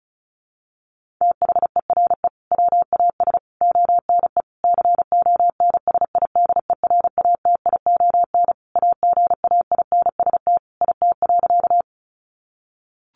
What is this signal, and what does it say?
Morse code, heard as a keyed tone, about 34 wpm, 700 Hz. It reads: THERE WAS ONE CONSIDERATION AGAINST IT.
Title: Sentences from Top 1500 Words 35wpm